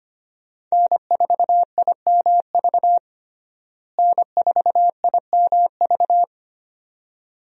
Morse code, heard as a keyed tone, about 25 wpm, 700 Hz. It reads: N4IMV N4IMV
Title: USA Call Signs - 5 Characters - Repeated 2x - 25wpm